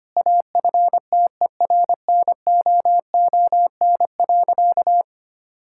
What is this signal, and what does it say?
Morse code, heard as a keyed tone, about 25 wpm, 700 Hz. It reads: AFTERNOON.
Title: Sentences from Top 1600 Words 25wpm